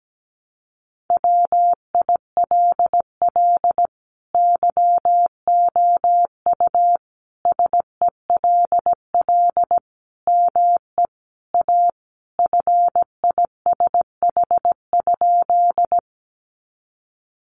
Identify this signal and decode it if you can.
Morse code, heard as a keyed tone, about 17 wpm, 700 Hz. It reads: WILL YOU SELL ME A FISH?
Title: Sentences from Top 1100 Words 17wpm